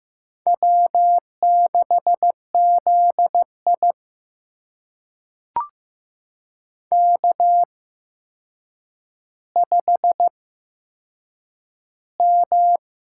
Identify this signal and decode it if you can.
Morse code, heard as a keyed tone, about 15 wpm, spaced as for 7 wpm, 700 Hz. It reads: W6ZI K5M
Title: USA Call Signs - 4 Characters - 8x Character Spacing - 15wpm